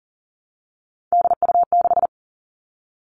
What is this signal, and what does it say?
Morse code, heard as a keyed tone, about 40 wpm, 700 Hz. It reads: DU6
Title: All Cut Numbers - 3-Digit 40wpm